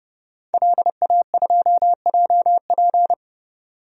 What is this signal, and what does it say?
Morse code, heard as a keyed tone, about 30 wpm, 700 Hz. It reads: LA2JP
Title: DXCC-100 Call Signs - 5 Characters - 30wpm